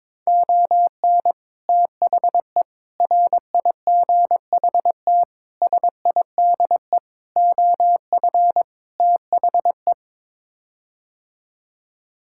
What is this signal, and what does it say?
Morse code, heard as a keyed tone, about 22 wpm, 700 Hz. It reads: ON THE RIGHT SIDE OF THE
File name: practice / 22wpm / Sets of 6 Words - 22wpm